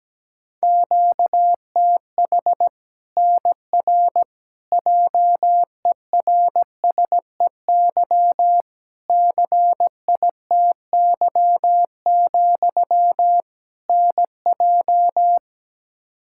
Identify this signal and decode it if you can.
Morse code, heard as a keyed tone, about 17 wpm, 700 Hz. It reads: QTH NR JERSEY CITY, NJ
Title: K1USN SST QSO Practice 17wpm